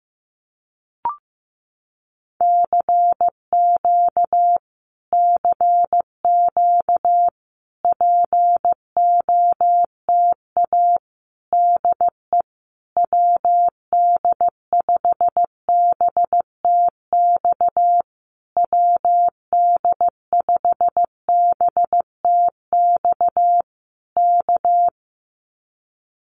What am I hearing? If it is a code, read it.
Morse code, heard as a keyed tone, about 15 wpm, 700 Hz. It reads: CQ CQ POTA DE WD5BTX WD5BTX K
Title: Common QSO Phrases Encore 15wpm